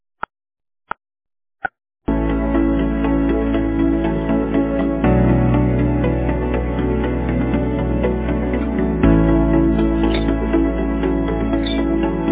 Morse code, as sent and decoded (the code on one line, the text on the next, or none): none